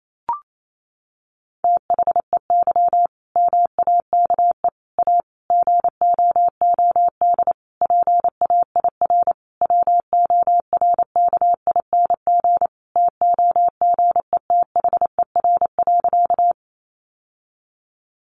Morse code, heard as a keyed tone, about 28 wpm, 700 Hz. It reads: THEY MAKE A GOOD PAIR WORKING TOGETHER.
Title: Sentences from Top 1200 Words 28wpm